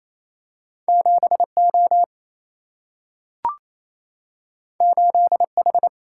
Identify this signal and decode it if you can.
Morse code, heard as a keyed tone, about 28 wpm, 700 Hz. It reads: 7O 8H
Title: Single-Letter Single-Number 28wpm